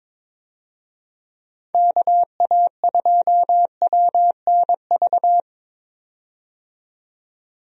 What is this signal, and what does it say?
Morse code, heard as a keyed tone, about 22 wpm, 700 Hz. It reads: KA2WNV
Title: USA Call Signs - 6 Characters - 22wpm